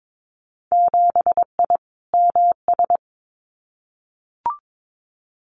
Morse code, heard as a keyed tone, about 22 wpm, 700 Hz. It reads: 7I MS
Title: Winter Field Day Exchanges Encore 22wpm